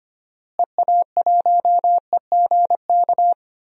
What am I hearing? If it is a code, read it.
Morse code, heard as a keyed tone, about 25 wpm, 700 Hz. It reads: EA1EGK
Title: DXCC-100 Call Signs - 6 Characters - Repeated 2x - 25wpm